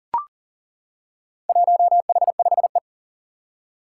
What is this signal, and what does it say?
Morse code, heard as a keyed tone, about 40 wpm, 700 Hz. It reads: 1H5E
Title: Four Challenging Letter-Number Combinations 40wpm